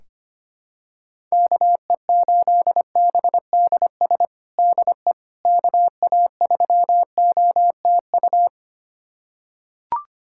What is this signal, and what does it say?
Morse code, heard as a keyed tone, about 25 wpm, 700 Hz. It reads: KE8BDS DE KA3OTU
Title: USA Callsign DE Callsign 25wpm